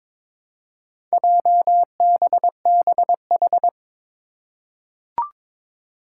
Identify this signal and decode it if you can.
Morse code, heard as a keyed tone, about 22 wpm, 700 Hz. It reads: JBBH
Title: Four Challenging Letter-Number Combinations 22wpm